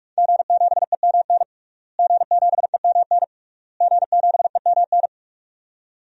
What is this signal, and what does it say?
Morse code, heard as a keyed tone, about 41 wpm, 700 Hz. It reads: G7EMN G7EMN G7EMN
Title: DXCC-100 Call Signs - 5 Characters - Repeated 3x - 45wpm